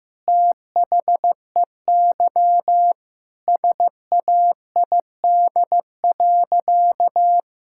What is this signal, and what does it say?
Morse code, heard as a keyed tone, about 16 wpm, 700 Hz. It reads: THEY SAID.
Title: Sentences from Top 1600 Words 15wpm